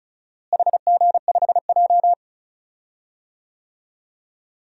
Morse code, heard as a keyed tone, about 36 wpm, 700 Hz. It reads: HG5J
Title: DXCC-100 Call Signs - 4 Characters - Repeated 3x - 35wpm